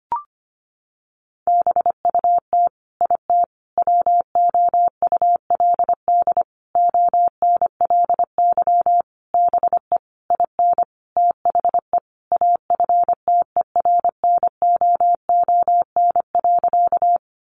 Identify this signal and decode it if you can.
Morse code, heard as a keyed tone, about 25 wpm, 700 Hz. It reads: BUT IT WOULD ONLY BE IN THE AFTERNOON.
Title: Sentences from Top 1500 Words 25wpm